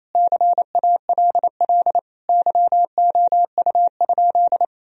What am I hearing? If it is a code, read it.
Morse code, heard as a keyed tone, about 28 wpm, 700 Hz. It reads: CALL YOU?
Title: Sentences from Top 1000 Words - Review Part 3 - 28wpm